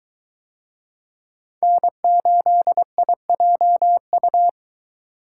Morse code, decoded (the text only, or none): N8IJU